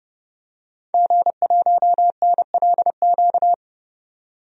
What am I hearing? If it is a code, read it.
Morse code, heard as a keyed tone, about 31 wpm, 700 Hz. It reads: G1NLQ
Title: DXCC-100 Call Signs - 5 Characters - 30wpm